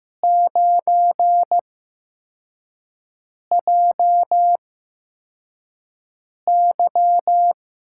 Morse code, heard as a keyed tone, about 15 wpm, spaced as for 4 wpm, 700 Hz. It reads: 9JY
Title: USA Call Signs - 6 Characters - 8x Character Spacing - 15wpm